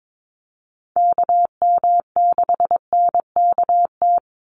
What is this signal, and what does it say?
Morse code, heard as a keyed tone, about 22 wpm, 700 Hz. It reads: KM6NKT
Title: USA Call Signs 22wpm